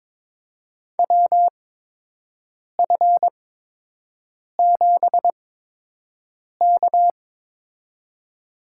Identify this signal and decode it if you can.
Morse code, heard as a keyed tone, about 21 wpm, spaced as for 6 wpm, 700 Hz. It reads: WF7K
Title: USA Call Signs - 4 Characters - 8x Character Spacing - 22wpm